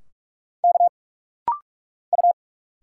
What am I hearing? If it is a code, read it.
Morse code, heard as a keyed tone, about 46 wpm, 700 Hz. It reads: K U
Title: Single Letters - Rapid-Fire 45wpm